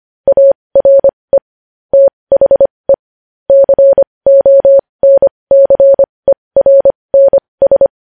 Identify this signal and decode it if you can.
Morse code, heard as a keyed tone, about 25 wpm, spaced as for 20 wpm, 550 Hz. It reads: ARE THE CONCERNS
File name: podcast / News headlines-1x-25x20